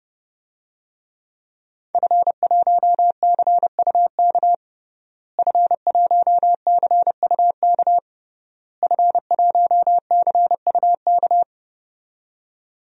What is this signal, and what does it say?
Morse code, heard as a keyed tone, about 32 wpm, 700 Hz. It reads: F1CUK F1CUK F1CUK
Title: DXCC-100 Call Signs - 5 Characters - Repeated 3x - 30wpm